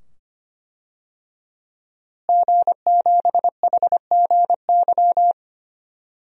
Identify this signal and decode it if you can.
Morse code, heard as a keyed tone, about 25 wpm, 700 Hz. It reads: G7HGY
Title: DXCC-100 Call Signs - 5 Characters - Repeated 2x - 25wpm